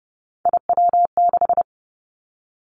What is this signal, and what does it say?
Morse code, heard as a keyed tone, about 30 wpm, 700 Hz. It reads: IW6
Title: Three Challenging Letter-Number Combinations 30wpm